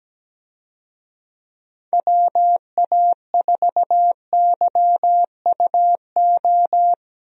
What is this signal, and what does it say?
Morse code, heard as a keyed tone, about 17 wpm, 700 Hz. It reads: WA4YUO
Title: USA Call Signs - 6 Characters - Repeated 2x - 17wpm